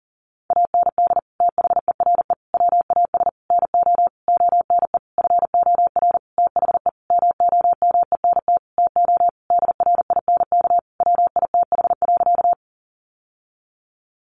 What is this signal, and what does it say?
Morse code, heard as a keyed tone, about 40 wpm, 700 Hz. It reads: AND THERE WAS NO ONE FOR THE MOMENT TO DRINK WITH.
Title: Sentences from Top 1500 Words 40wpm